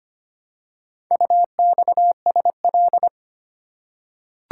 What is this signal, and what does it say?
Morse code, heard as a keyed tone, about 25 wpm, 700 Hz. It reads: UXSL
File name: practice / 25wpm / Four Letter-Number Combinations - 25wpm